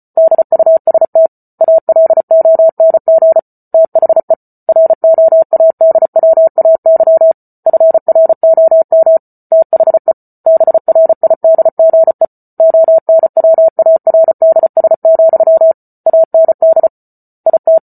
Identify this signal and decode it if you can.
Morse code, heard as a keyed tone, about 34 wpm, 650 Hz. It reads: DUST ALONG THE ROADWAY FROM THE BRIDGE ONWARDS, AND IT